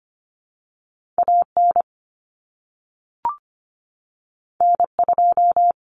Morse code, heard as a keyed tone, about 25 wpm, 700 Hz. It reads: AN N2
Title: Common Cut Numbers - 2-Digit 25wpm